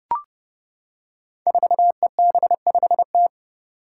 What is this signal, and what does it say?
Morse code, heard as a keyed tone, about 30 wpm, 700 Hz. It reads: 4EB5T
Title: Five Challenging Letter-Number Combinations 30wpm